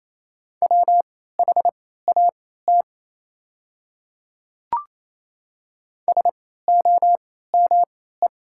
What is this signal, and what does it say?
Morse code, heard as a keyed tone, about 28 wpm, spaced as for 16 wpm, 700 Hz. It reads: WHAT SOME
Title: 3x Character Spacing - Top 100 Words - 28wpm